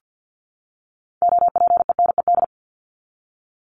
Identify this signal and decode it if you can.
Morse code, heard as a keyed tone, about 50 wpm, 700 Hz. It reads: OPENED
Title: ED Suffix - Top 100 Words with Base Words 50wpm